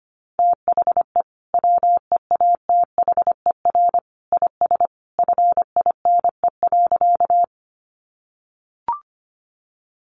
Morse code, heard as a keyed tone, about 25 wpm, 700 Hz. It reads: THE WEATHER IS FINE.
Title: Sentences from Top 800 Words 25wpm